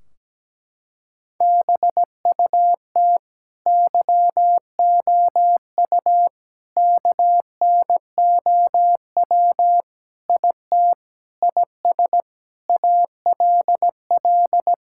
Morse code, heard as a keyed tone, about 17 wpm, 700 Hz. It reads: BUT YOU KNOW IT IS ALL
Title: Sentences from Top 1600 Words 17wpm